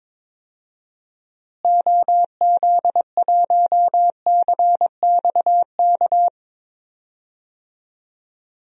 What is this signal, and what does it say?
Morse code, heard as a keyed tone, about 22 wpm, 700 Hz. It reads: OZ1CXK
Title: DXCC-100 Call Signs - 6 Characters - Repeated 3x - 22wpm